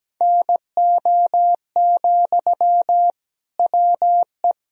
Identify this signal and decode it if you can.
Morse code, heard as a keyed tone, about 17 wpm, 700 Hz. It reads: NO, WE
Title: Sentences from Top 1000 Words - Review Part 3 - 17wpm